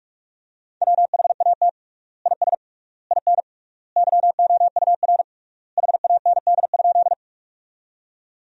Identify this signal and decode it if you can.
Morse code, heard as a keyed tone, about 46 wpm, 700 Hz. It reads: WHAT IS IN YOUR HAND?
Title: 3x Word-Spacing - Sentences from Top 200 Words - 45wpm